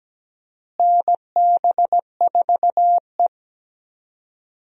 Morse code, heard as a keyed tone, about 17 wpm, 700 Hz. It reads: NB4E
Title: USA Call Signs - 4 Characters - Repeated 2x - 17wpm